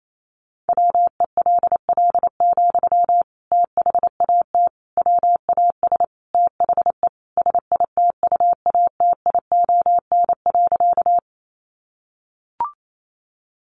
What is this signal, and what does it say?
Morse code, heard as a keyed tone, about 28 wpm, 700 Hz. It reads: WELL, THAT WAS THE SITUATION.
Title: Sentences from Top 1200 Words 28wpm